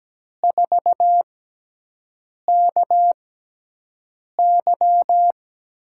Morse code, heard as a keyed tone, about 17 wpm, spaced as for 6 wpm, 700 Hz. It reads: 4KY
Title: USA Call Signs - 6x Character Spacing - 17wpm